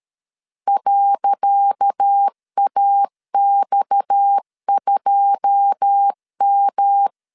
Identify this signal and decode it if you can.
Morse code, heard as a keyed tone, about 13 wpm, 800 Hz. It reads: .AX2M
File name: LEZIONI / 10 SIGNS / 10C DOT 12wpm 2451